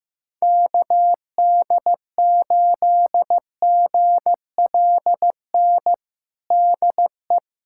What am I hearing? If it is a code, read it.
Morse code, heard as a keyed tone, about 16 wpm, 700 Hz. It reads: KD8GLN DE
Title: USA Callsign DE Callsign 15wpm